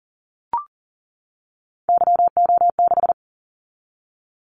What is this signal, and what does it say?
Morse code, heard as a keyed tone, about 37 wpm, 700 Hz. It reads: YO6
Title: Callsign Prefixes for Top 25 DXCC - 40wpm